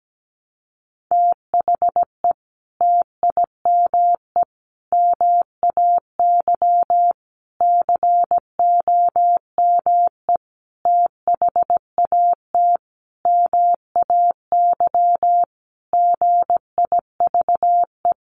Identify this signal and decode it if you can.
Morse code, heard as a keyed tone, about 17 wpm, 700 Hz. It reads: THE TIME MAY COME THAT MAY GIVE
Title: Sentences from Top 1100 Words 17wpm